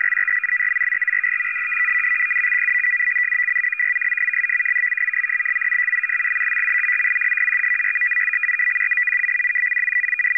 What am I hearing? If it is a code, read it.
Two-tone teleprinter signal, about 450 Hz shift.
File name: POL- -0900and0905utc-fsk100-500